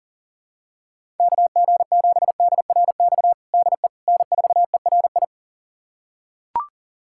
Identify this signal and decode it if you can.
Morse code, heard as a keyed tone, about 40 wpm, 700 Hz. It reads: KG7DRX DE N4ERI
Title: USA Callsign DE Callsign 40wpm